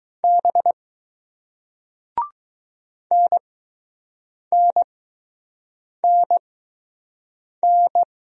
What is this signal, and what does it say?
Morse code, heard as a keyed tone, about 20 wpm, spaced as for 6 wpm, 700 Hz. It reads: B NNNN